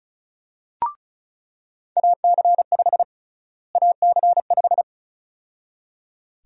Morse code, heard as a keyed tone, about 34 wpm, 700 Hz. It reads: AC5 AC5
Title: USA Call Sign Prefixes - Repeated 2x - 35wpm